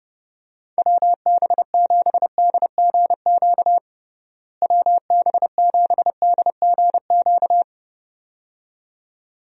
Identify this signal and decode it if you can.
Morse code, heard as a keyed tone, about 30 wpm, 700 Hz. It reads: WB7DGQ WB7DGQ
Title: USA Call Signs - 6 Characters - Repeated 2x - 30wpm